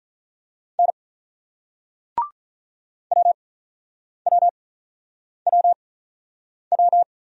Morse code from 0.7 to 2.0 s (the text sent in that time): N